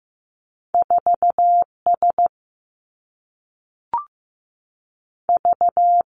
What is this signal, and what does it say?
Morse code, heard as a keyed tone, about 15 wpm, 700 Hz. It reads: 4 S V
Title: Two Challenging Letter-Number Combinations 15wpm